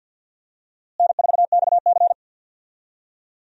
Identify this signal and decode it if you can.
Morse code, heard as a keyed tone, about 52 wpm, 700 Hz. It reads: N4XC
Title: USA Call Signs - 4 Characters - Repeated 3x - 50wpm